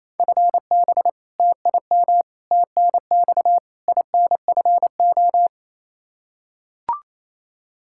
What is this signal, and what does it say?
Morse code, heard as a keyed tone, about 28 wpm, 700 Hz. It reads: FB TIM TNX INFO
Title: Practice QSO - Exchange 3 - More Info - 28wpm